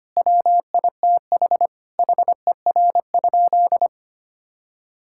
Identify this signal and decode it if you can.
Morse code, heard as a keyed tone, about 25 wpm, 700 Hz. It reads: WITH HER?
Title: Sentences from Top 1000 Words - Review Part 6 - 25wpm